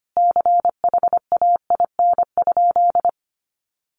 Morse code, heard as a keyed tone, about 25 wpm, 700 Hz. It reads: CHAIN?
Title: Sentences from Top 1500 Words 25wpm